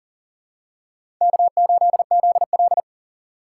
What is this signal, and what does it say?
Morse code, heard as a keyed tone, about 37 wpm, 700 Hz. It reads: K8ZL
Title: USA Call Signs - 3x Character Spacing - 40wpm